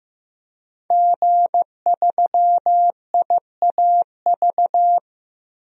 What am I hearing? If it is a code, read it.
Morse code, heard as a keyed tone, about 15 wpm, 700 Hz. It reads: G3IAV
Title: DXCC-100 Call Signs - 5 Characters - 15wpm